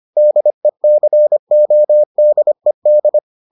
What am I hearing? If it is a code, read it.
Morse code, heard as a keyed tone, about 25 wpm, 600 Hz. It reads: DECODED